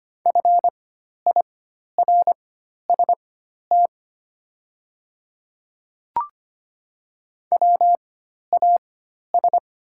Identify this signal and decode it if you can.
Morse code, heard as a keyed tone, about 25 wpm, spaced as for 12 wpm, 700 Hz. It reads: FIRST WAS
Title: 4x Character Spacing - Top 100 Words - 25wpm